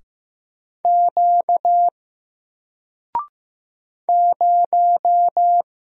Morse code, heard as a keyed tone, about 15 wpm, 700 Hz. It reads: Q 0
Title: Single Character (Letter:Number:Punctuation:Prosign) 15wpm